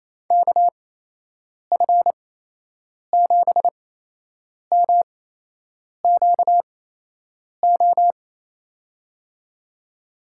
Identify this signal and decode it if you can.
Morse code, heard as a keyed tone, about 28 wpm, spaced as for 8 wpm, 700 Hz. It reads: KF7MQO
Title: USA Call Signs - 6 Characters - 8x Character Spacing - 28wpm